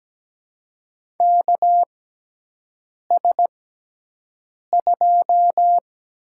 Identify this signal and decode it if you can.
Morse code, heard as a keyed tone, about 17 wpm, spaced as for 6 wpm, 700 Hz. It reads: KS2
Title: USA Call Signs - 6x Character Spacing - 17wpm